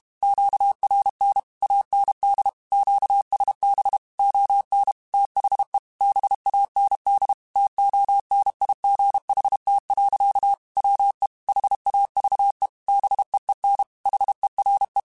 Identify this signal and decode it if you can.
Morse code, heard as a keyed tone, about 32 wpm, 800 Hz. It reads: QRN AND QSB ON THE BAND TONIGHT. WE HAVE BEEN HERE
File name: QSOs / QSOs-mp3s / WD0FBT 1@30